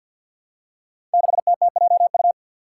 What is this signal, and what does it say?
Morse code, heard as a keyed tone, about 53 wpm, 700 Hz. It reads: 6TTJU
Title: Five Challenging Letter-Number Combinations 50wpm